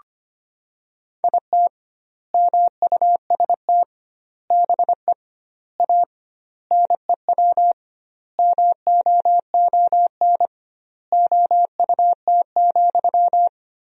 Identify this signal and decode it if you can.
Morse code, heard as a keyed tone, about 25 wpm, 700 Hz. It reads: IT MUST BE A NEW MOON OUT,
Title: 2x Word-Spacing - Sentences from Top 500 Words - 25wpm